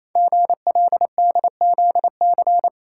Morse code, heard as a keyed tone, about 29 wpm, 700 Hz. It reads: GLDZC
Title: Code Groups - 5 Letters - 28wpm